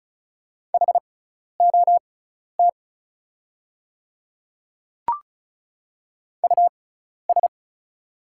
Morse code, heard as a keyed tone, about 36 wpm, spaced as for 12 wpm, 700 Hz. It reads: HOT US
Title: 6x Character Spacing - Top 200 Words - 35wpm